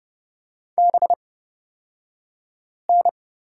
Morse code, heard as a keyed tone, about 30 wpm, spaced as for 5 wpm, 700 Hz. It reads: BN
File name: Lesson 37d - Letter x - Two Characters 30wpm